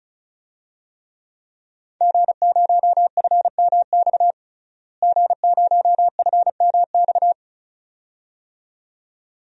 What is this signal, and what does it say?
Morse code, heard as a keyed tone, about 36 wpm, 700 Hz. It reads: G0FMX G0FMX
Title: DXCC-100 Call Signs - 5 Characters - Repeated 2x - 35wpm